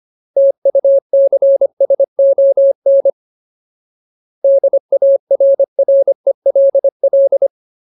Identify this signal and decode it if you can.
Morse code, heard as a keyed tone, about 25 wpm, 550 Hz. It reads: TUCSON DARRELL
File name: Ultimate Word List / 25WPM Ultimate